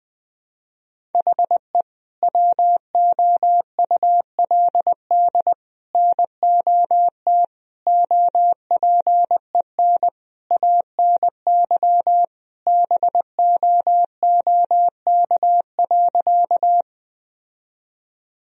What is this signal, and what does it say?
Morse code, heard as a keyed tone, about 21 wpm, 700 Hz. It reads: HE WOULD NOT OPEN ANY BOOK.
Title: Sentences from Top 1000 Words - Review Part 1 - 20wpm